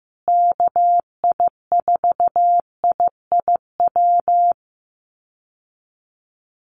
Morse code, heard as a keyed tone, about 14 wpm, 700 Hz. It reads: KI4IIW
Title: USA Call Signs Repeated 3x 15wpm